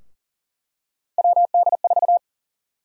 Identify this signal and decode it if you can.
Morse code, heard as a keyed tone, about 40 wpm, 700 Hz. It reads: WD4
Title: USA Call Sign Prefixes - Repeated 2x - 40wpm